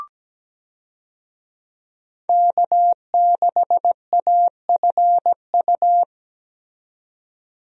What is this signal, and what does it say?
Morse code, heard as a keyed tone, about 17 wpm, 700 Hz. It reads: K6AFU
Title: USA Call Signs - 5 Characters - 17wpm